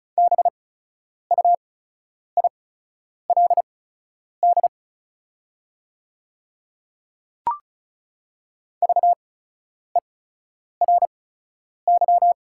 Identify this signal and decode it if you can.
Morse code, heard as a keyed tone, about 35 wpm, spaced as for 10 wpm, 690 Hz. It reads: BUILD VERY
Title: 8x Character Spacing - Top 200 Words - 35wpm